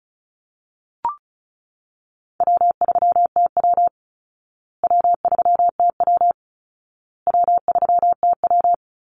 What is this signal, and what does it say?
Morse code, heard as a keyed tone, about 34 wpm, 700 Hz. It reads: W3TW W3TW W3TW
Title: Easy Call Signs Repeated 3x - 35wpm